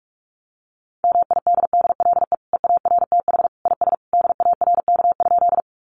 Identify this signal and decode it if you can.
Morse code, heard as a keyed tone, about 44 wpm, 700 Hz. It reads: MIDDLE EARTH IS DARK?
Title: Sentences from Top 900 Words 45wpm